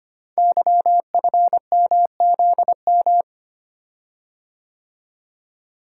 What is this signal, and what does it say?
Morse code, heard as a keyed tone, about 25 wpm, 700 Hz. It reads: YFMZM
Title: Code Groups - 5 Letters - 25wpm